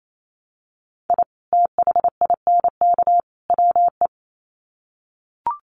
Morse code, heard as a keyed tone, about 28 wpm, 700 Hz. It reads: I THINK WE
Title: Sets of 3 Words 28wpm